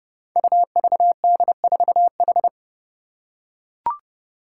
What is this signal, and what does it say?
Morse code, heard as a keyed tone, about 30 wpm, 700 Hz. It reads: UVD4H
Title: Five Challenging Letter-Number Combinations 30wpm